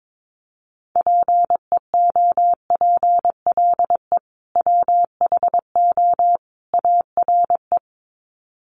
Morse code, heard as a keyed tone, about 22 wpm, 700 Hz. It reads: PEOPLE WHO ARE
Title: Sets of 3 Words 22wpm